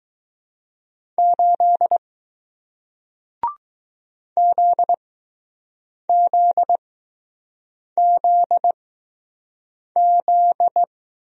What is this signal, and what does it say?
Morse code, heard as a keyed tone, about 20 wpm, spaced as for 6 wpm, 700 Hz. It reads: 8 ZZZZ